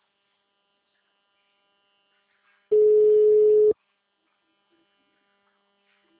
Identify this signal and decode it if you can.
Ringback tone (Europe, 425 Hz), with speech.